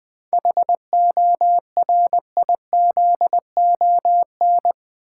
Morse code, heard as a keyed tone, about 20 wpm, 700 Hz. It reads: HORIZON
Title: 7 Letter Words - First 3 Letters Given - 20wpm